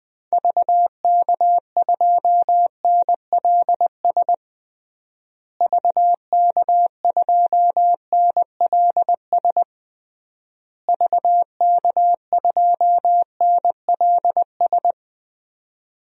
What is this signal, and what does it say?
Morse code, heard as a keyed tone, about 21 wpm, 700 Hz. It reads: VK2NLS VK2NLS VK2NLS
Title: DXCC-100 Call Signs - 6 Characters - Repeated 3x - 20wpm